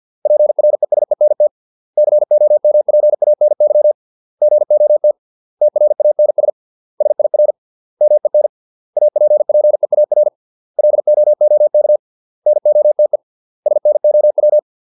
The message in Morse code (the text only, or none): PRESENT COMPANY GOT TRANS SIR MEN APPEAR LOOK NOTE SNOW